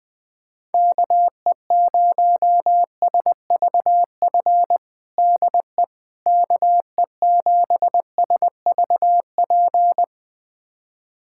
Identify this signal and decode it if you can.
Morse code, heard as a keyed tone, about 20 wpm, 700 Hz. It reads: KE0SVF DE KE7SVP
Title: USA Callsign DE Callsign - 6 Characters - 20wpm